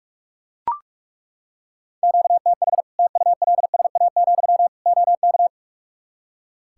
Morse code, heard as a keyed tone, about 41 wpm, 700 Hz. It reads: QTH TULSA, OK
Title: Practice QSO - Exchange 1 - Beginning - 45wpm